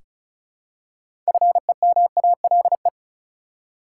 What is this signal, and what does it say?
Morse code, heard as a keyed tone, about 36 wpm, 700 Hz. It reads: FEMALE
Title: Six Letter Words - 35wpm